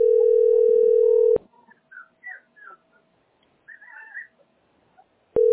Ringback tone (North America), 440 + 480 Hz.
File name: fb605bbd-e4db-4365-87bf